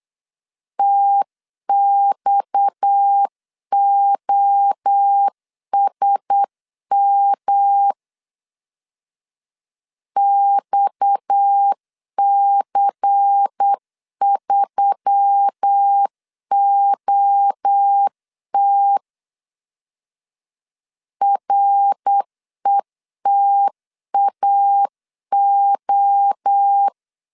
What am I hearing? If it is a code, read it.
Morse code, heard as a keyed tone, about 8 wpm, 800 Hz. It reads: TXOSM XC3OT RETAO
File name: CW / DET / 1 6 1500C S C 8wpm 5219